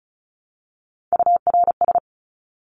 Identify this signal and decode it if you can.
Morse code, heard as a keyed tone, about 34 wpm, 700 Hz. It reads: URS
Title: Abbreviations Repeated 3x - 35wpm